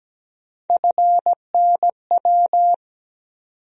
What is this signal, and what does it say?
Morse code, heard as a keyed tone, about 17 wpm, 700 Hz. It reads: FNW
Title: USA Call Sign Postfixes - 3 Letters 17wpm